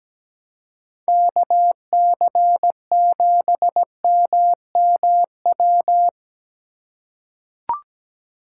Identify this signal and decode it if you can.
Morse code, heard as a keyed tone, about 17 wpm, 700 Hz. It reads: KC7MMW
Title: USA Call Signs - 6 Characters - 4x Character Spacing - 17wpm